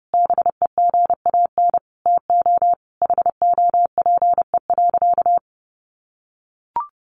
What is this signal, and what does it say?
Morse code, heard as a keyed tone, about 30 wpm, 700 Hz. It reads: BEGAN TO HOPE.
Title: Sentences from Top 1100 Words 30wpm